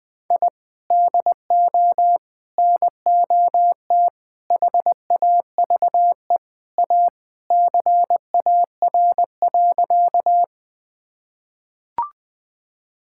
Morse code, heard as a keyed tone, about 20 wpm, 700 Hz. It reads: I DO NOT HAVE A CAR.
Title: Sentences with 5-Word Phrases - 20wpm